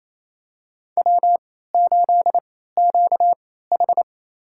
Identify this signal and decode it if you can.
Morse code, heard as a keyed tone, about 28 wpm, spaced as for 16 wpm, 700 Hz. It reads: W8QH
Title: USA Call Signs - 4 Characters - 3x Character Spacing - 28wpm